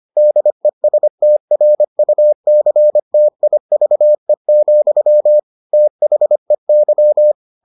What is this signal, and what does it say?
Morse code, heard as a keyed tone, about 25 wpm, 600 Hz. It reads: DESTRUCTIVE, THEY